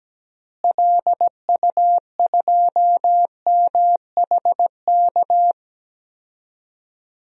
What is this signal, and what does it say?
Morse code, heard as a keyed tone, about 17 wpm, 700 Hz. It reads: LU2MHK
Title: DXCC-100 Call Signs - 6 Characters - Repeated 3x - 17wpm